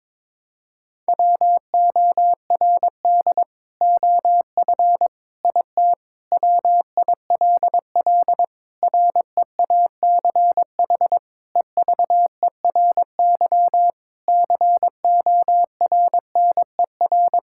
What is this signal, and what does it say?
Morse code, heard as a keyed tone, about 22 wpm, 700 Hz. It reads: WORD OF IT WILL REACH EVERY CORNER
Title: Sentences from Top 1600 Words 22wpm